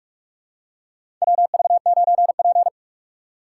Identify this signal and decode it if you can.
Morse code, heard as a keyed tone, about 46 wpm, 690 Hz. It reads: WV9P
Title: USA Call Signs - 4 Characters - 4x Character Spacing - 45wpm